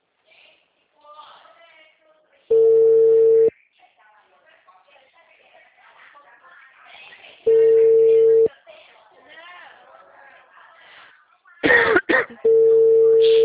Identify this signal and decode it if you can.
Ringback tone (China, 450 Hz), with speech.
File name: out- -2002- - - .7338